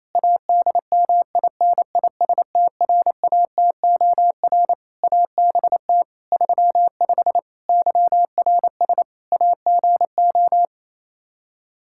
Morse code, heard as a keyed tone, about 27 wpm, 700 Hz. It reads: ADMINISTRATOR ABT 35 YRS AGO
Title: Practice QSO - Exchange 3 - More Info - 28wpm